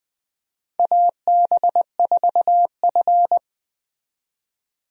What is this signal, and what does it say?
Morse code, heard as a keyed tone, about 21 wpm, 700 Hz. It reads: AB4F
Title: USA Call Signs - 4 Characters - 20wpm